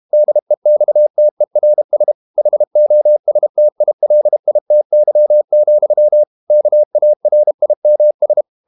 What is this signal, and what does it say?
Morse code, heard as a keyed tone, about 32 wpm, 600 Hz. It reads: DEXTERS HOSTILITY, KARIMS
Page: Never 31wpm | CW med Gnister